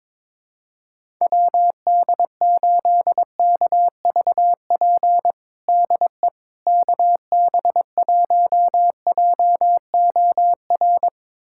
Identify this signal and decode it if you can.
Morse code, heard as a keyed tone, about 22 wpm, 700 Hz. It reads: WD8KVP DE KB1JOR
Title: USA Callsign DE Callsign - 6 Characters - 22wpm